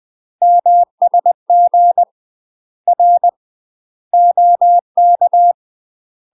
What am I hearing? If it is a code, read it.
Morse code, heard as a keyed tone, about 20 wpm, 700 Hz. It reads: MSG R OK
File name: UK MASTER QSOs TRAINING / 20WPM / UK MASTER QSOs 0003